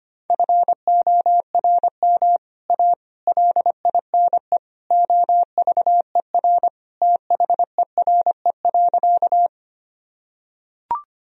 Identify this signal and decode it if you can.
Morse code, heard as a keyed tone, about 25 wpm, 700 Hz. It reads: FORM A LINE OVER THERE.
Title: Sentences from Top 1000 Words - Review Part 1 - 25wpm